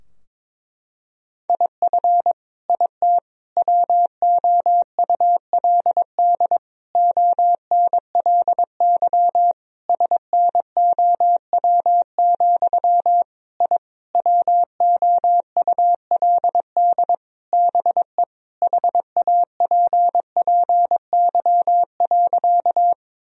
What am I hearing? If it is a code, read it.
Morse code, heard as a keyed tone, about 22 wpm, 700 Hz. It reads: IF IT WOULD ONLY SNOW, I WOULD BE HAPPY.
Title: Sentences from Top 1000 Words - Review Part 3 - 22wpm